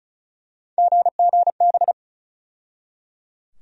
Morse code, heard as a keyed tone, about 34 wpm, 700 Hz. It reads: GGB
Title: USA Call Sign Postfixes - 3 Letters 35wpm